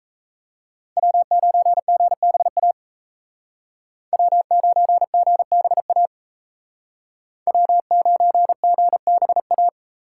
Morse code, heard as a keyed tone, about 36 wpm, 700 Hz. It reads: W9GBA W9GBA W9GBA